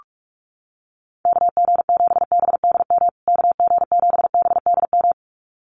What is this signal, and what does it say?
Morse code, heard as a keyed tone, about 46 wpm, 700 Hz. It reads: KG7BDM KG7BDM
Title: USA Call Signs Repeated 2x 45wpm